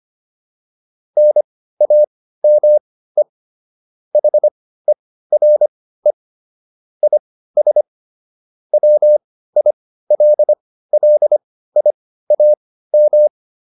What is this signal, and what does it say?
Morse code, heard as a keyed tone, about 25 wpm, spaced as for 15 wpm, 600 Hz. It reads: NAME HERE IS WILLIAM